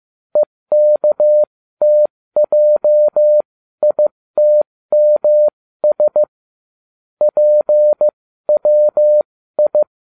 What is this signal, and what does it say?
Morse code, heard as a keyed tone, about 15 wpm, spaced as for 13 wpm, 600 Hz. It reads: EKTJITMS PWI